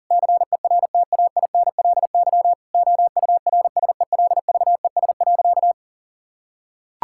Morse code, heard as a keyed tone, about 40 wpm, 700 Hz. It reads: CERTAINLY OURSELVES.
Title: Sentences from Top 1600 Words 40wpm